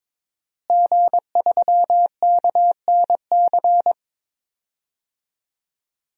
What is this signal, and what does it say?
Morse code, heard as a keyed tone, about 22 wpm, 700 Hz. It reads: G3KNC
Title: DXCC-100 Call Signs - 5 Characters - Repeated 3x - 22wpm